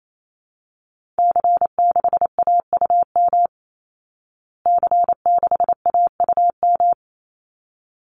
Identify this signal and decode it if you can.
Morse code, heard as a keyed tone, about 28 wpm, 700 Hz. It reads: C6AUM C6AUM
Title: Call Signs Repeated 3x - 28wpm